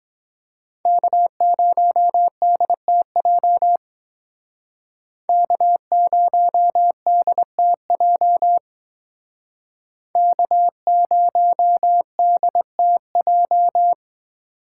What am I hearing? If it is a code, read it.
Morse code, heard as a keyed tone, about 22 wpm, 700 Hz. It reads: K0DTJ K0DTJ K0DTJ